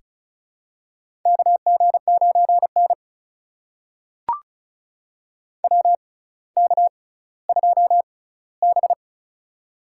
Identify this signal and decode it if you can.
Morse code, heard as a keyed tone, about 36 wpm, spaced as for 18 wpm, 700 Hz. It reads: KG9N WK2B
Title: USA Call Signs - 4 Characters - 6x Character Spacing - 35wpm